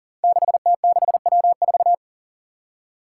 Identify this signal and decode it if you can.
Morse code, heard as a keyed tone, about 40 wpm, 700 Hz. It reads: 6T6W4
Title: Five Challenging Letter-Number Combinations 40wpm